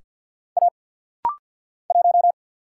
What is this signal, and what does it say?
Morse code, heard as a keyed tone, about 50 wpm, 700 Hz. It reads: A 1
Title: Single Letter-Number - Rapid-Fire 50wpm